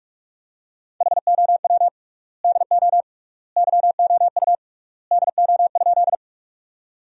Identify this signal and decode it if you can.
Morse code, heard as a keyed tone, about 46 wpm, 700 Hz. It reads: HOW DO YOU DO?
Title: 3x Word-Spacing - Sentences from Top 100 Words - 45wpm